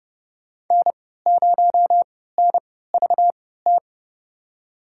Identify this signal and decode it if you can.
Morse code, heard as a keyed tone, about 32 wpm, spaced as for 17 wpm, 700 Hz. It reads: N0NVT
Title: USA Call Signs - 5 Characters - 3x Character Spacing - 30wpm